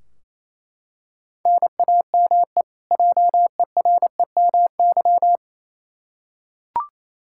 Morse code, heard as a keyed tone, about 28 wpm, 700 Hz. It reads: NAME JEREMY
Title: K1USN SST QSO Practice 28wpm